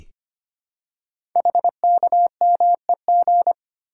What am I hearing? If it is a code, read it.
Morse code, heard as a keyed tone, about 25 wpm, 690 Hz. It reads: HKMEG
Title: Code Groups - 5 Letters - 25wpm